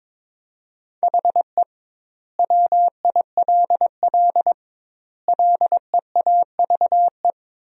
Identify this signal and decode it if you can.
Morse code, heard as a keyed tone, about 22 wpm, 700 Hz. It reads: HE WILL LEAVE
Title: 2x Word-Spacing - Sentences from Top 400 Words - 22wpm